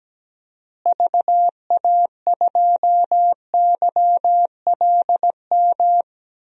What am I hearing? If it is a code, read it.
Morse code, heard as a keyed tone, about 17 wpm, 700 Hz. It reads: VA2YLM
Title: DXCC-100 Call Signs - 6 Characters - Repeated 2x - 17wpm